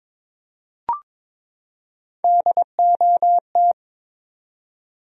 Morse code, heard as a keyed tone, about 22 wpm, 700 Hz. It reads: DOT
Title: USA Common Names - Max 4 Letters - 22wpm